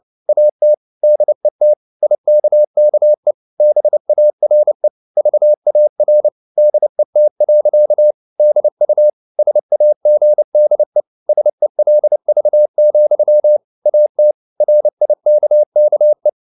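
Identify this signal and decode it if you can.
Morse code, heard as a keyed tone, about 29 wpm, 600 Hz. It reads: AT DET IKKE BARE VAR DET. DU SAGDE SELV, AT RIKKE
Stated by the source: Fast (QRQ) Morse at 29 wpm, from CW med Gnister.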